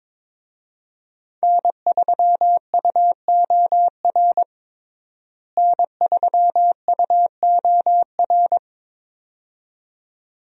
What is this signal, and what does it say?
Morse code, heard as a keyed tone, about 22 wpm, 700 Hz. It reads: N3UOR N3UOR
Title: USA Call Signs - 5 Characters - Repeated 2x - 22wpm